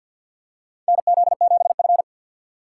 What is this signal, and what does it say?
Morse code, heard as a keyed tone, about 52 wpm, 700 Hz. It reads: NZ7F
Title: USA Call Signs - 4 Characters - 6x Character Spacing - 50wpm